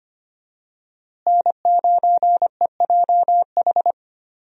Morse code, heard as a keyed tone, about 25 wpm, 700 Hz. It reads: N9EJH
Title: USA Call Signs - 5 Characters - Repeated 3x - 25wpm